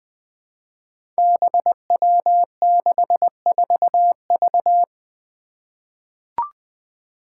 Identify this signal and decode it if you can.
Morse code, heard as a keyed tone, about 21 wpm, 700 Hz. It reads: BW64V
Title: Five Challenging Letter-Number Combinations 20wpm